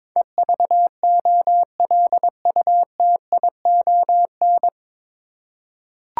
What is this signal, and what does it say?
Morse code, heard as a keyed tone, about 22 wpm, 700 Hz. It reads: EVOLUTION
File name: 9 Letter Words - First 3 Letters Given - 22wpm